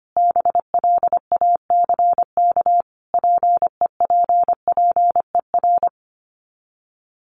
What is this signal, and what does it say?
Morse code, heard as a keyed tone, about 25 wpm, 700 Hz. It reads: BLACK PEPPER
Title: Sets of 5 Words Encore 25wpm